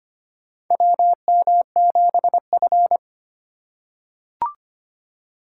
Morse code, heard as a keyed tone, about 25 wpm, 700 Hz. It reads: WM7F
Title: USA Call Signs - 4 Characters - 2x Character Spacing - 25wpm